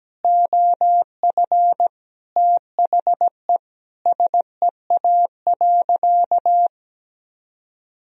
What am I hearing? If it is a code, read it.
Morse code, heard as a keyed tone, about 17 wpm, 700 Hz. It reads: OF THE SEA.
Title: Sentences from Top 1000 Words - Review Part 1 - 17wpm